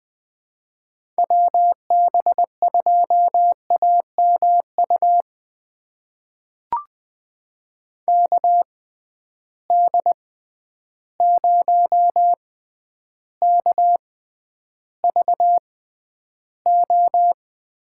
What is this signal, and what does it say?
Morse code, heard as a keyed tone, about 20 wpm, spaced as for 10 wpm, 700 Hz. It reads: WB2AMU KD0KVO
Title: USA Call Signs - 6 Characters - 6x Character Spacing - 20wpm